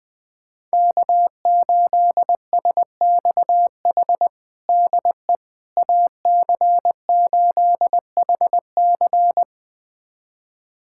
Morse code, heard as a keyed tone, about 20 wpm, 700 Hz. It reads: K8SXH DE AC8HC
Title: USA Callsign DE Callsign 20wpm